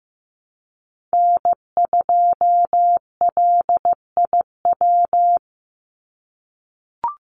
Morse code, heard as a keyed tone, about 15 wpm, 700 Hz. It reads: N2LIW
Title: USA Call Signs 15wpm